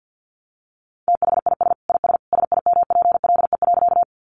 Morse code, heard as a keyed tone, about 50 wpm, 700 Hz. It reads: THIS IS SIMPLE.
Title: Sentences from Top 400 Words 50wpm